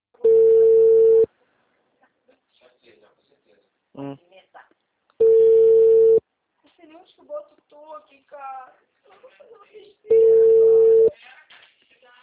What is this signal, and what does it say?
Ringback tone (China, 450 Hz), with speech.
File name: out- -2010